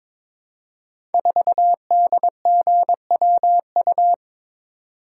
Morse code, heard as a keyed tone, about 22 wpm, 700 Hz. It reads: 4DGWU
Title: Code Groups - 5 Letter:Numbers - 22wpm